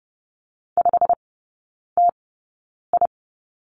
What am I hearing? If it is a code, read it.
Morse code, heard as a keyed tone, about 30 wpm, spaced as for 9 wpm, 700 Hz. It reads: 5TI
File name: course / 30:10wpm / 087 Lesson 15e - Number 5 - Three Characters 30wpm